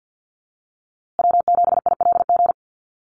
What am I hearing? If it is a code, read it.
Morse code, heard as a keyed tone, about 48 wpm, 700 Hz. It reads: W7ILG
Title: USA Call Signs Repeated 3x 50wpm